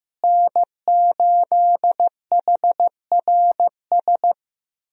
Morse code, heard as a keyed tone, about 15 wpm, 700 Hz. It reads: N8HRS
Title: USA Call Signs - 5 Characters - Repeated 3x - 15wpm